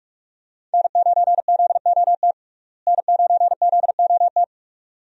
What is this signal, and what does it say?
Morse code, heard as a keyed tone, about 47 wpm, 700 Hz. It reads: N9ZOT N9ZOT
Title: USA Call Signs - 5 Characters - Repeated 2x - 45wpm